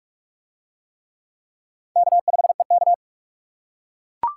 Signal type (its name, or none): Morse code